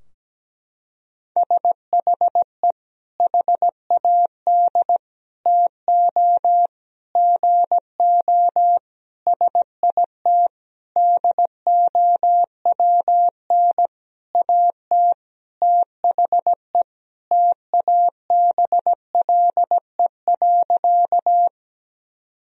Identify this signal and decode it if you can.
Morse code, heard as a keyed tone, about 17 wpm, 700 Hz. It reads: SHE HAD TO GO SIT DOWN AT THE TABLE.
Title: Sentences from Top 1000 Words - Review Part 2 - 17wpm